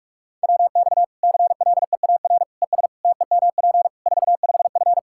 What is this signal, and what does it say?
Morse code, heard as a keyed tone, about 46 wpm, 700 Hz. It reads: WX CLEAR ES TEMP 45F
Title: Practice QSO - Exchange 2 - Info - 45wpm